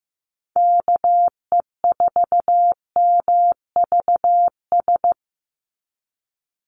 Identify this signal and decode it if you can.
Morse code, heard as a keyed tone, about 15 wpm, 700 Hz. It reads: KE4MVS
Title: USA Call Signs 15wpm